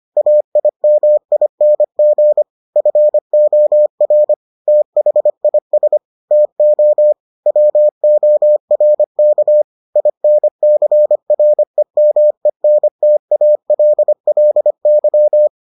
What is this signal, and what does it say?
Morse code, heard as a keyed tone, about 25 wpm, 600 Hz. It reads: AIMING FOR THIS TO WORK INCREMENTALLY